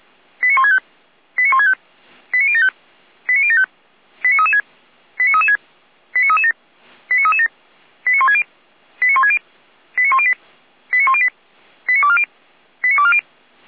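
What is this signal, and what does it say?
Five-tone selective call (ZVEI), heard as quick runs of five tones, tones of about 70 ms. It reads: 89165 89165 89075 89075 89336 89336 89337 89337 89160 89160 89108 89108 89240 89240
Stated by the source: ZVEI five-tone selective call received by SDR.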